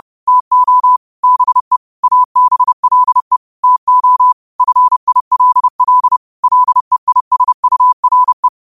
Morse code, heard as a keyed tone, about 30 wpm, 1000 Hz. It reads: TO BE ABLE TO FILL LEISURE
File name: qotdfiles / 30wpm